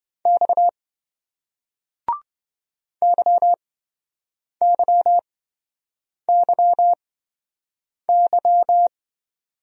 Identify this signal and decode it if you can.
Morse code, heard as a keyed tone, about 28 wpm, spaced as for 7 wpm, 700 Hz. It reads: X YYYY